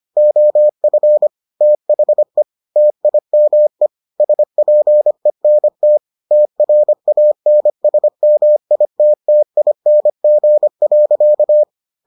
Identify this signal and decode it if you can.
Morse code, heard as a keyed tone, about 25 wpm, 600 Hz. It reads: OF THE TIME SPENT TRANSMITTING.